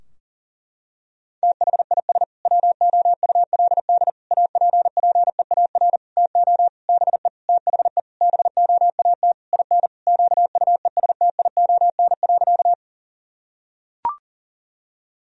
Morse code, heard as a keyed tone, about 40 wpm, 700 Hz. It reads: THIS WOULD APPEAR TO BE THE BOAT IN QUESTION.
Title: Sentences from Top 1000 Words - Review Part 3 - 40wpm